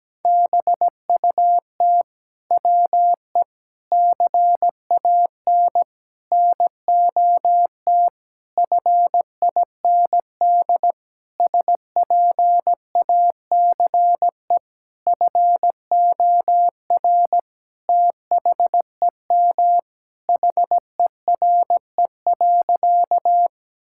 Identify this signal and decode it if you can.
Morse code, heard as a keyed tone, about 17 wpm, 700 Hz. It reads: BUT WE CAN NOT FIND SPACE FOR THEM HERE.
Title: Sentences from Top 1000 Words - Review Part 5 - 17wpm